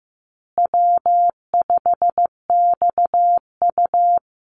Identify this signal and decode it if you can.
Morse code, heard as a keyed tone, about 15 wpm, 700 Hz. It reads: W5XU
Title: Easy Call Signs Repeated 3x - 15wpm